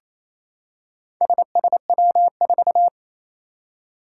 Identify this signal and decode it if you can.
Morse code, heard as a keyed tone, about 27 wpm, 700 Hz. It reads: SSW4
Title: Four Challenging Letter-Number Combinations 28wpm